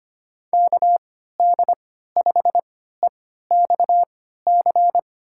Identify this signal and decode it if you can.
Morse code, heard as a keyed tone, about 25 wpm, spaced as for 14 wpm, 700 Hz. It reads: KD5EXC
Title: USA Call Signs - 3x Character Spacing - 25wpm